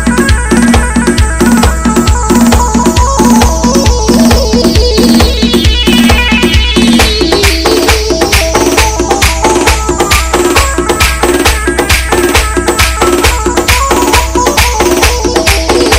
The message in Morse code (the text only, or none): none